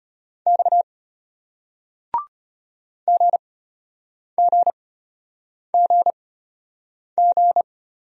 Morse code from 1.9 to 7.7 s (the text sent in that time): GGGG